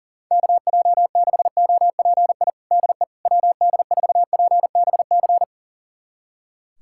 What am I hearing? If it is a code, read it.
Morse code, heard as a keyed tone, about 40 wpm, 700 Hz. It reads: KJ6OPI DE WD4PBC
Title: USA Callsign DE Callsign - 6 Characters - 40wpm